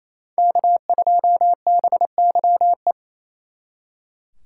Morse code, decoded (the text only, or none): K2BYE